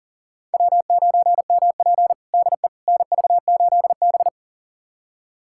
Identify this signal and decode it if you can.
Morse code, heard as a keyed tone, about 40 wpm, 700 Hz. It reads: W9MP DE NV8B
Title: USA Callsign DE Callsign - 4 Characters - 40wpm